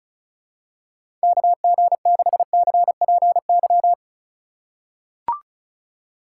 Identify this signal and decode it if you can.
Morse code, heard as a keyed tone, about 34 wpm, 700 Hz. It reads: KG6CPY
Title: USA Call Signs - 6 Characters - 3x Character Spacing - 35wpm